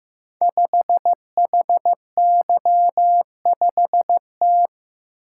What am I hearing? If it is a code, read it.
Morse code, heard as a keyed tone, about 15 wpm, 700 Hz. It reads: 5HY5T
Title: Code Groups - 5 Letter:Numbers - 15wpm